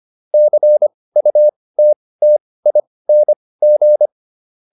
Morse code, heard as a keyed tone, about 25 wpm, spaced as for 18 wpm, 600 Hz. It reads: CUTTING